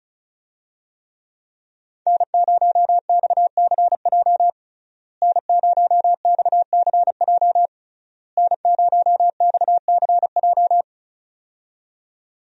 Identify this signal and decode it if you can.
Morse code, heard as a keyed tone, about 36 wpm, 700 Hz. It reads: N0XCJ N0XCJ N0XCJ
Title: USA Call Signs - 5 Characters - Repeated 3x - 35wpm